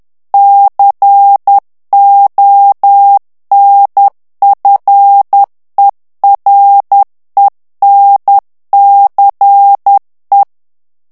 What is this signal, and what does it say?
Morse code, heard as a keyed tone, about 10 wpm, 800 Hz. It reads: CONFERENCE
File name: ANGLAIS texte clair 3 vitesse 10 mots minute